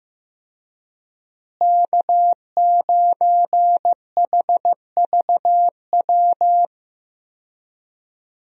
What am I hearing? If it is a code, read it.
Morse code, heard as a keyed tone, about 15 wpm, 700 Hz. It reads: K9HVW
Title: USA Call Signs - 5 Characters - Repeated 2x - 15wpm